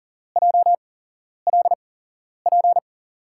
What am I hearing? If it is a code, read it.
Morse code, heard as a keyed tone, about 40 wpm, spaced as for 11 wpm, 700 Hz. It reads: JLP
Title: USA Call Signs - 6 Characters - 8x Character Spacing - 40wpm